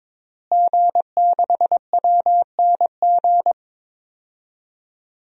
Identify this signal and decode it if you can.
Morse code, heard as a keyed tone, about 21 wpm, 700 Hz. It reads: G6WNG
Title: DXCC-100 Call Signs - 5 Characters - Repeated 3x - 22wpm